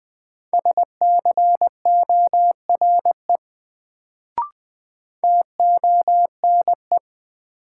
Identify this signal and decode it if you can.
Morse code, heard as a keyed tone, about 20 wpm, 700 Hz. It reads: SCORE TONE
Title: Sentences from Top 1600 Words - New Word Drill - 20wpm